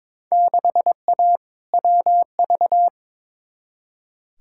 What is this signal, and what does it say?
Morse code, heard as a keyed tone, about 22 wpm, 700 Hz. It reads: 6A WV
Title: Field Day Exchanges - 22wpm